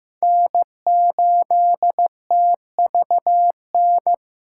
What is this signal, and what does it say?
Morse code, heard as a keyed tone, about 15 wpm, 700 Hz. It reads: N8TVN
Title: USA Callsign DE Callsign 15wpm